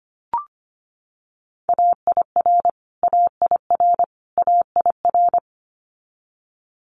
Morse code, heard as a keyed tone, about 25 wpm, 700 Hz. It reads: AIR AIR AIR
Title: Top 200 Words Repeated 25wpm